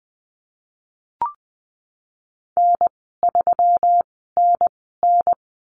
Morse code, heard as a keyed tone, about 20 wpm, spaced as for 15 wpm, 700 Hz. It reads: N3NN
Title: 186 Lesson 30g - Letter b - Call Signs 20wpm